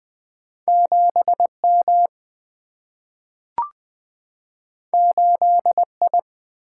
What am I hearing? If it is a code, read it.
Morse code, heard as a keyed tone, about 20 wpm, 700 Hz. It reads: 7 M 8 I
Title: Single-Letter Single-Number 20wpm